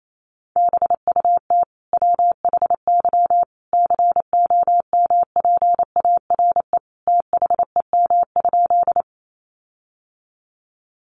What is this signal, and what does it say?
Morse code, heard as a keyed tone, about 28 wpm, 700 Hz. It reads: BUT WHY COMPARE THEM?
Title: Sentences from Top 1300 Words 28wpm